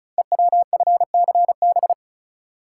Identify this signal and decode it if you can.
Morse code, heard as a keyed tone, about 36 wpm, 690 Hz. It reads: EWFCB
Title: Code Groups - 5 Letters - 35wpm